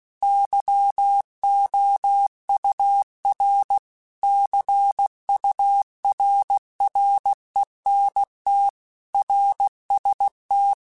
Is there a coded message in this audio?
Yes: Morse code from 0.2 to 10.7 s, heard as a keyed tone, about 16 wpm, 800 Hz. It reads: YOUR CURRENT RST